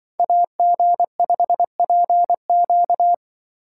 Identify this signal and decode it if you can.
Morse code, heard as a keyed tone, about 25 wpm, 700 Hz. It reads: AG5PQ